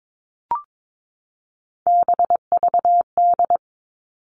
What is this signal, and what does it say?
Morse code, heard as a keyed tone, about 22 wpm, 700 Hz. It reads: BVD
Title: Three Challenging Letter-Number Combinations 22wpm